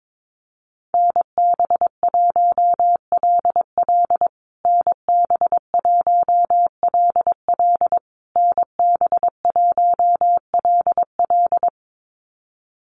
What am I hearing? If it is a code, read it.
Morse code, heard as a keyed tone, about 22 wpm, 700 Hz. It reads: NB1LL NB1LL NB1LL
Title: USA Call Signs Repeated 3x 22wpm